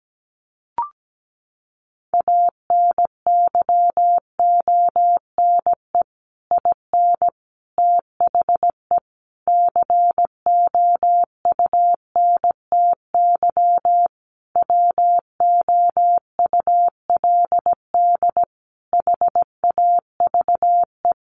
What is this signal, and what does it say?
Morse code, heard as a keyed tone, about 17 wpm, 700 Hz. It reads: ANYONE IN THE COUNTY WOULD HAVE
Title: Sentences from Top 1200 Words 17wpm